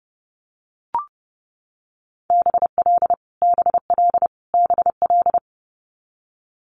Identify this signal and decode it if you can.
Morse code, heard as a keyed tone, about 30 wpm, 700 Hz. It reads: BL BL BL
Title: Multi-letter Phonemes Repeated 3x 30wpm